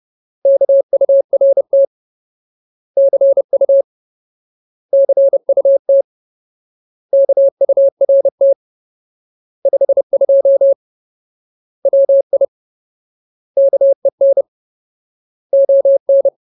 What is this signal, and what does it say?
Morse code, heard as a keyed tone, about 30 wpm, 550 Hz. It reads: KURT CU CUT KURT 52 WI KEN ON
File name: Ultimate Word List / 30WPM Ultimate